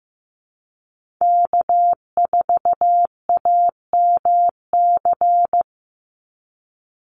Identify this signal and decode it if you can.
Morse code, heard as a keyed tone, about 15 wpm, 700 Hz. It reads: K4AMC
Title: Call Signs Repeated 2x - 15wpm